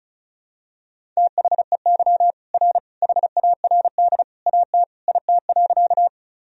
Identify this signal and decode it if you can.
Morse code, heard as a keyed tone, about 34 wpm, 700 Hz. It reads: THEY R HARD AT IT.
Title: Sentences from Top 300 Words and Abbreviations 35wpm